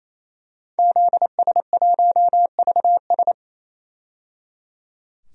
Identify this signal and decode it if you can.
Morse code, heard as a keyed tone, about 28 wpm, 700 Hz. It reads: ZS1VS
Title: DXCC-100 Call Signs - 5 Characters - 28wpm